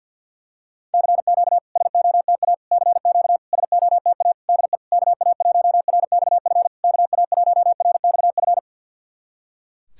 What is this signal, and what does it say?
Morse code, heard as a keyed tone, about 52 wpm, 690 Hz. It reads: CQ SOTA CQ SOTA DE KA1RXF KA1RXF
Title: SOTA QSO Practice 50wpm